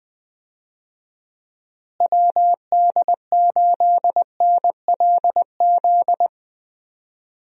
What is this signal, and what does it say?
Morse code, heard as a keyed tone, about 21 wpm, 700 Hz. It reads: WD8NLZ
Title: USA Call Signs - 6 Characters - Repeated 2x - 20wpm